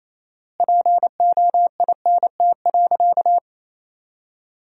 Morse code, heard as a keyed tone, about 28 wpm, 700 Hz. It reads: POINT.
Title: 4x Word-Spacing - Sentences from Top 200 Words - 28wpm